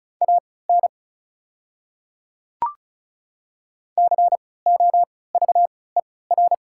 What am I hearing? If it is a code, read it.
Morse code, heard as a keyed tone, about 34 wpm, spaced as for 20 wpm, 700 Hz. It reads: AN COVER
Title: 3x Character Spacing - Top 300 Words - 35wpm